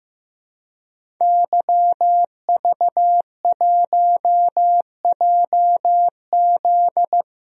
Morse code, heard as a keyed tone, about 15 wpm, 700 Hz. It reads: YV1JZ
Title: DXCC-100 Call Signs - 5 Characters - Repeated 3x - 15wpm